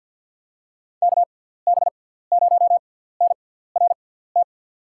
Morse code, heard as a keyed tone, about 47 wpm, spaced as for 17 wpm, 700 Hz. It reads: KB0NRT
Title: USA Call Signs - 6 Characters - 6x Character Spacing - 50wpm